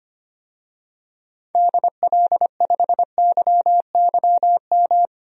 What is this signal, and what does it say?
Morse code, heard as a keyed tone, about 25 wpm, 700 Hz. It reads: DL5YYM
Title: DXCC-100 Call Signs - 6 Characters - 25wpm